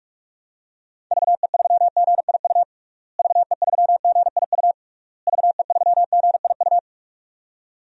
Morse code, heard as a keyed tone, about 41 wpm, 700 Hz. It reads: VE3GIU VE3GIU VE3GIU
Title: DXCC-100 Call Signs - 6 Characters - Repeated 3x - 45wpm